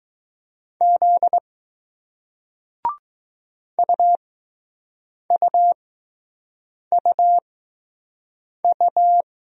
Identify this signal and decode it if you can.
Morse code, heard as a keyed tone, about 20 wpm, spaced as for 6 wpm, 700 Hz. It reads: Z UUUU